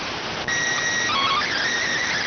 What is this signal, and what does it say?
SSTV (slow-scan television), PD 290.